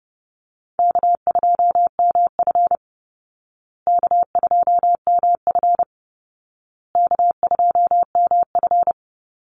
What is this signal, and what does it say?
Morse code, heard as a keyed tone, about 28 wpm, 700 Hz. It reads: K2MF K2MF K2MF
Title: Easy Call Signs Repeated 3x - 30wpm